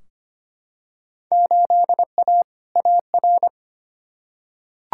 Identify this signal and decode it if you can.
Morse code, heard as a keyed tone, about 25 wpm, 700 Hz. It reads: 8A AR
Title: Field Day Exchanges Encore - 25wpm